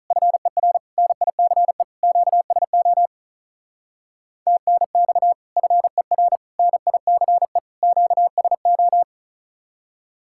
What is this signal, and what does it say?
Morse code, heard as a keyed tone, about 47 wpm, 700 Hz. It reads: FER NICE QSO TNX FER NICE QSO